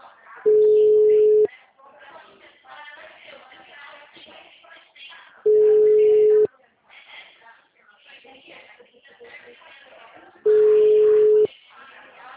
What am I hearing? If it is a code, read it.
Ringback tone (Europe, 425 Hz), with speech.